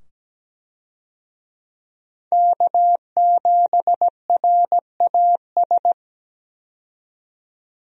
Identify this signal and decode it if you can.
Morse code, heard as a keyed tone, about 17 wpm, 700 Hz. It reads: K7RAS
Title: USA Call Signs - 5 Characters - 17wpm